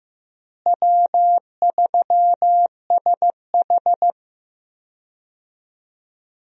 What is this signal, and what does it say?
Morse code, heard as a keyed tone, about 16 wpm, 700 Hz. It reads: W3SH
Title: USA Call Signs - 4 Characters - Repeated 3x - 15wpm